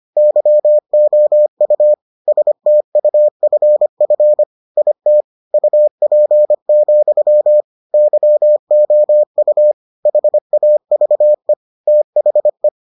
Morse code, heard as a keyed tone, about 25 wpm, 600 Hz. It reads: YOU STUFF IT UP, YOU HAVE THE